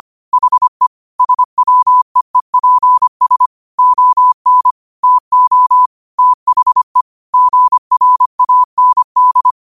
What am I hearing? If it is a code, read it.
Morse code, heard as a keyed tone, about 25 wpm, 1000 Hz. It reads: HE SWEEPS ON TO THE GRAND
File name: qotdfiles / 25wpm